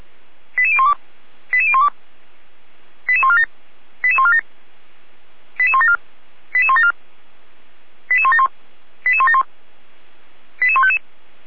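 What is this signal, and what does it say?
Five-tone selective call (ZVEI), heard as quick runs of five tones, tones of about 80 ms. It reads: 88012 88012 88147 88147 88174 88174 88171 88171 88155
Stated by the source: ZVEI five-tone selective call received by SDR, fire brigade Orlamünde (feuerwehr-orlamuende).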